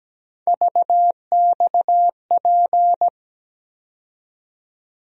Morse code, heard as a keyed tone, about 17 wpm, 700 Hz. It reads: VXP
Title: USA Call Sign Postfixes - 3 Letters 17wpm